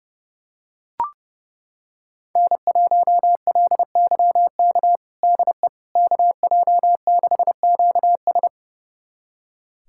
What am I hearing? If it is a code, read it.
Morse code, heard as a keyed tone, about 30 wpm, 700 Hz. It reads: N1LYK DE KJ6QS
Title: USA Callsign DE Callsign - 5 Characters - 30wpm